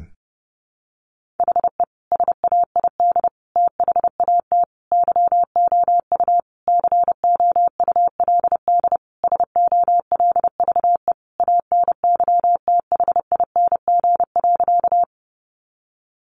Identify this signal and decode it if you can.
Morse code, heard as a keyed tone, about 30 wpm, 700 Hz. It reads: HE SAID THAT YOU COULD SOLVE ANYTHING.
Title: Sentences from Top 1000 Words 30wpm